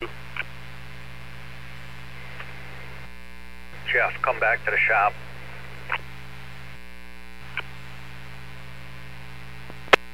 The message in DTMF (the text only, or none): none